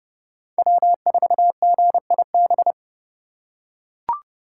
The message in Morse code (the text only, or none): W4GIB